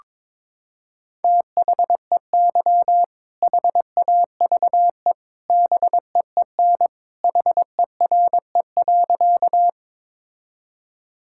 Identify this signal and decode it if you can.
Morse code, heard as a keyed tone, about 22 wpm, 700 Hz. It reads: THEY HAVE BEEN HERE.
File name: practice / 22wpm / Sentences with 2-Word Phrases - 22wpm